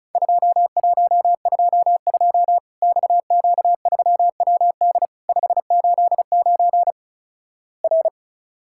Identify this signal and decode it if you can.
Morse code, heard as a keyed tone, about 36 wpm, 700 Hz. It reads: 2122 XQ3WD 589 R
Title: Worked All Europe DX Contest - QTC Exchange - 35wpm